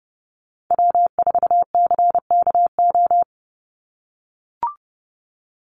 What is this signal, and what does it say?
Morse code, heard as a keyed tone, about 30 wpm, 700 Hz. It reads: W4CKO
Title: USA Call Signs 30wpm